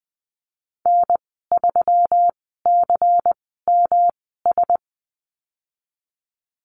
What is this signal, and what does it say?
Morse code, heard as a keyed tone, about 20 wpm, spaced as for 15 wpm, 700 Hz. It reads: N3CMS
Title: 128 Lesson 21g - Number 6 - Call Signs 20wpm